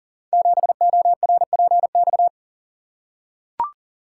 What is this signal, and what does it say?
Morse code, heard as a keyed tone, about 38 wpm, 700 Hz. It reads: 7ORPX
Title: Code Groups - 5 Letter:Numbers - 40wpm